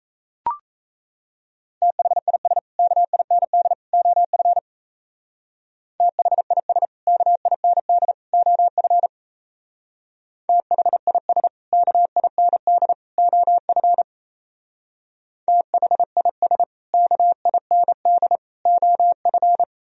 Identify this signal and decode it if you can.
Morse code, heard as a keyed tone, about 36 wpm, 700 Hz. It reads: THIS KIND OF THIS KIND OF THIS KIND OF THIS KIND OF